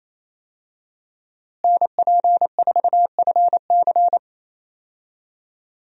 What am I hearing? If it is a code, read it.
Morse code, heard as a keyed tone, about 28 wpm, 700 Hz. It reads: NP4FC
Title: USA Call Signs - 5 Characters - Repeated 3x - 28wpm